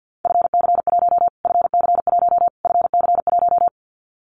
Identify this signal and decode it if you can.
Morse code, heard as a keyed tone, about 50 wpm, 700 Hz. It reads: FC1 FC1 FC1
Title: Callsign Prefixes for Top 25 DXCC - Repeated 3x - 50wpm